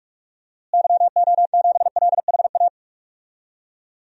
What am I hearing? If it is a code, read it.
Morse code, heard as a keyed tone, about 45 wpm, 700 Hz. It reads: YO7LHA
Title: DXCC-100 Call Signs - 6 Characters - Repeated 3x - 45wpm